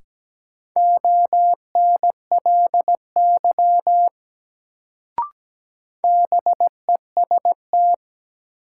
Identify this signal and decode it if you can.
Morse code, heard as a keyed tone, about 17 wpm, 700 Hz. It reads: ONLY BEST
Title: Sets of 2 Words - New Word Drill - 17wpm